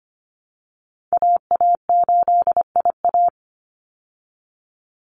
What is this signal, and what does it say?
Morse code, heard as a keyed tone, about 25 wpm, 700 Hz. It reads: AA8IA
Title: Call Signs Repeated 3x - 25wpm